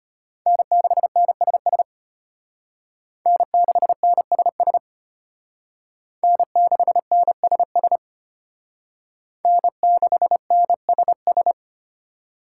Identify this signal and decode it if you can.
Morse code, heard as a keyed tone, about 32 wpm, 700 Hz. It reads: N6NSS N6NSS N6NSS N6NSS